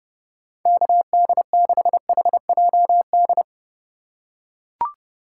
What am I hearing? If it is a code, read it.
Morse code, heard as a keyed tone, about 32 wpm, 700 Hz. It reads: KD6HJD
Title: USA Call Signs - 6 Characters - 3x Character Spacing - 30wpm